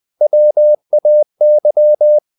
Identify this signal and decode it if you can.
Morse code, heard as a keyed tone, about 20 wpm, 600 Hz. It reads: WAY
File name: the old man and the sea / the old man and the sea part 1 20wpm